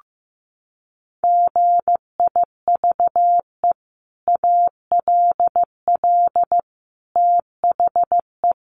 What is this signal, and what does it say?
Morse code, heard as a keyed tone, about 15 wpm, 700 Hz. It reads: GIVE ALL THE
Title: Sentences from Top 600 Words 15wpm